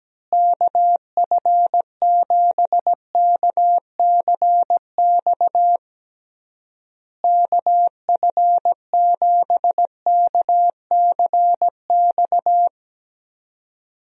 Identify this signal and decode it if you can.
Morse code, heard as a keyed tone, about 17 wpm, 700 Hz. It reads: KF7KCX KF7KCX
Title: USA Call Signs - 6 Characters - Repeated 3x - 17wpm